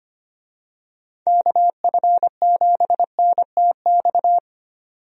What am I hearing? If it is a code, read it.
Morse code, heard as a keyed tone, about 25 wpm, 700 Hz. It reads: KF7NTX
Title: USA Call Signs - 6 Characters - 25wpm